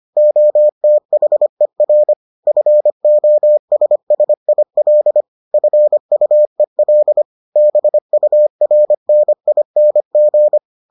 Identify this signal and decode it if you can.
Morse code, heard as a keyed tone, about 25 wpm, 600 Hz. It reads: OTHER FOSSIL FUEL BURNING